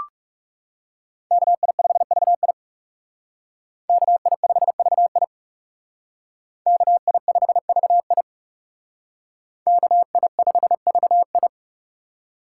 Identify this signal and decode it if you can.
Morse code, heard as a keyed tone, about 38 wpm, 700 Hz. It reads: KI5VI KI5VI KI5VI KI5VI